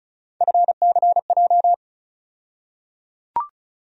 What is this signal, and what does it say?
Morse code, heard as a keyed tone, about 34 wpm, 700 Hz. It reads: FCJ
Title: USA Call Sign Postfixes - 3 Letters 35wpm